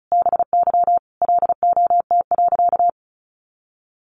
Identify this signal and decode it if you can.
Morse code, heard as a keyed tone, about 34 wpm, 700 Hz. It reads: BY LOT.
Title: Sentences from Top 1500 Words 35wpm